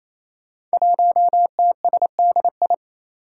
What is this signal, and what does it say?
Morse code, heard as a keyed tone, about 28 wpm, 700 Hz. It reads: 1TSDI
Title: Five Challenging Letter-Number Combinations 28wpm